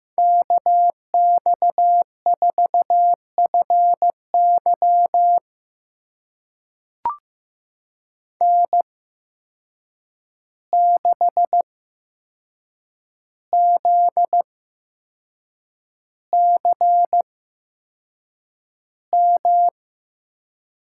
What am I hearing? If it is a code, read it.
Morse code, heard as a keyed tone, about 15 wpm, spaced as for 6 wpm, 700 Hz. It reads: KX4FY N6ZCM
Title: USA Call Signs - 5 Characters - 8x Character Spacing - 15wpm